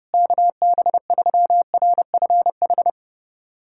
Morse code, heard as a keyed tone, about 31 wpm, 700 Hz. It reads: KB3RFH
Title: USA Callsign DE Callsign 30wpm